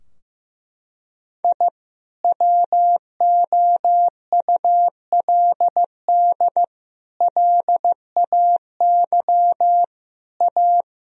Morse code, heard as a keyed tone, about 15 wpm, 700 Hz. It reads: I WOULD LAY A
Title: Sentences from Top 1000 Words - Review Part 3 - 15wpm